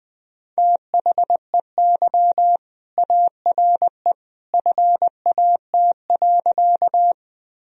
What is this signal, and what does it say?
Morse code, heard as a keyed tone, about 20 wpm, 700 Hz. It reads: THEY ARE FAT.
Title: Sentences from Top 1000 Words - Review Part 4 - 20wpm